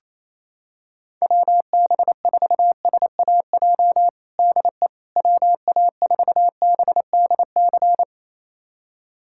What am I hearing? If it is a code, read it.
Morse code, heard as a keyed tone, about 28 wpm, 700 Hz. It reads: WB4SAJ DE WA4BDC
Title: USA Callsign DE Callsign - 6 Characters - 28wpm